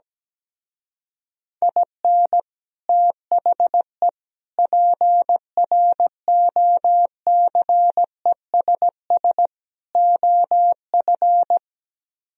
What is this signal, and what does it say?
Morse code, heard as a keyed tone, about 17 wpm, 700 Hz. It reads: IN THE PROCESS OF
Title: Speed-Racing - Sets of 4 Words 17wpm